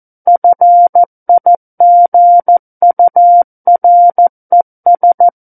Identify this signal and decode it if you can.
Morse code, heard as a keyed tone, about 14 wpm, 700 Hz. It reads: FIGURES
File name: Book One / 05-Chapter 5 - 14 WPM